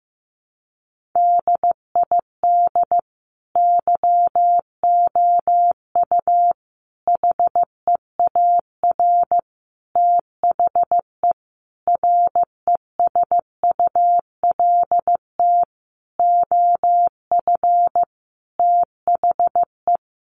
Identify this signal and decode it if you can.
Morse code, heard as a keyed tone, about 14 wpm, 700 Hz. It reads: DID YOU HEAR THE RESULT OF THE
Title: Sentences from Top 1400 Words 15wpm